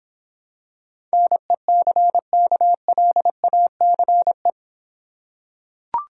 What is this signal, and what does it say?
Morse code, heard as a keyed tone, about 26 wpm, 700 Hz. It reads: NECKLACE